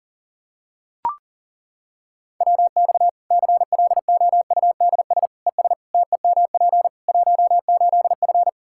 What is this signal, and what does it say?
Morse code, heard as a keyed tone, about 40 wpm, 700 Hz. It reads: WX CLOUDS ES TEMP 18F
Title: Practice QSO - Exchange 2 - Info - 40wpm